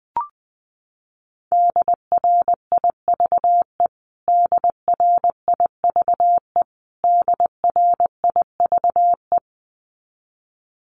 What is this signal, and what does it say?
Morse code, heard as a keyed tone, about 20 wpm, 700 Hz. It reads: DRIVE DRIVE DRIVE
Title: Top 500 Words Repeated 20wpm